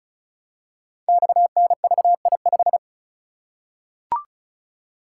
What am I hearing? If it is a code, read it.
Morse code, heard as a keyed tone, about 36 wpm, 700 Hz. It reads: XNVI5
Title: Code Groups - 5 Letter:Numbers - 35wpm